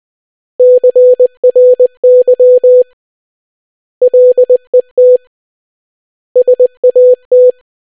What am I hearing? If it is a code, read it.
Morse code, heard as a keyed tone, about 20 wpm, 500 Hz. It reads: CRY LET SAT